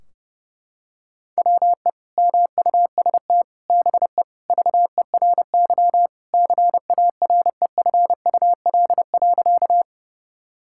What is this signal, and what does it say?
Morse code, heard as a keyed tone, about 32 wpm, 700 Hz. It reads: WE MUST BE VERY CAREFUL.
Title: Sentences from Top 1600 Words 30wpm